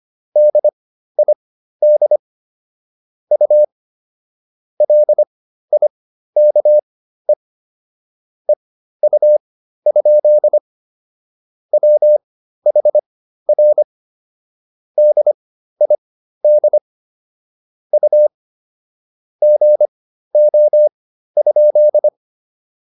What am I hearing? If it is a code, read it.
Morse code, heard as a keyed tone, about 25 wpm, spaced as for 13 wpm, 600 Hz. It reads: DID U LIKE EU? WHR DID U GO?